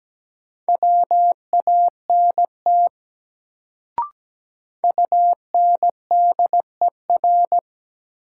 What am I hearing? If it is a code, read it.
Morse code, heard as a keyed tone, about 17 wpm, 700 Hz. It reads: WANT UNDER
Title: Sets of 2 Words - New Word Drill - 17wpm